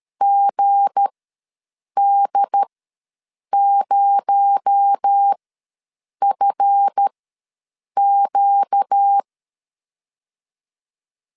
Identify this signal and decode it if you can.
Morse code, heard as a keyed tone, about 13 wpm, spaced as for 7 wpm, 800 Hz. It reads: GD0FQ
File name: LEZIONI / 9 2867 ws 12wpm 3431